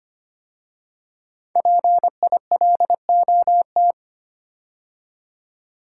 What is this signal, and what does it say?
Morse code, heard as a keyed tone, about 25 wpm, 700 Hz. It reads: PILOT
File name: 5 Letter Words - First 3 Letters Given - 25wpm